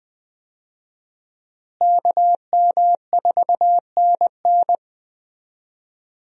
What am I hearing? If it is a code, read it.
Morse code, heard as a keyed tone, about 21 wpm, 700 Hz. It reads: KM4NN
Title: USA Call Signs - 5 Characters - Repeated 2x - 20wpm